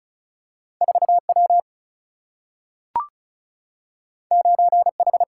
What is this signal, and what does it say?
Morse code, heard as a keyed tone, about 36 wpm, 700 Hz. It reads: 4 W 9 H
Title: Single-Letter Single-Number 35wpm